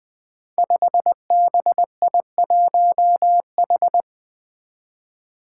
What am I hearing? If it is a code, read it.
Morse code, heard as a keyed tone, about 20 wpm, 700 Hz. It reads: 5BI1H
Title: Five Challenging Letter-Number Combinations 20wpm